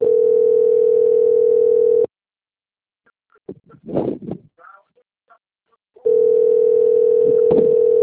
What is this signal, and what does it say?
Ringback tone (North America), 440 + 480 Hz.